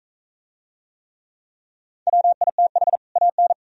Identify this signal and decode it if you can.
Morse code, heard as a keyed tone, about 42 wpm, 690 Hz. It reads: WITH AN